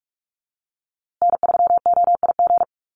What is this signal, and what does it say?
Morse code, heard as a keyed tone, about 44 wpm, 700 Hz. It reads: N3OIG
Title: USA Call Signs Repeated 2x 45wpm